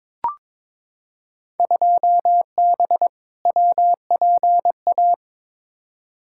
Morse code, heard as a keyed tone, about 22 wpm, 700 Hz. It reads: 2B WPA
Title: Field Day Exchanges Encore - 22wpm